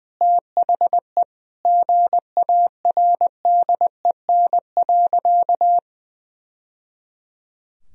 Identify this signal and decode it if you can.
Morse code, heard as a keyed tone, about 20 wpm, 700 Hz. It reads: THE GARDEN.
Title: Sentences from Top 1000 Words - Review Part 4 - 20wpm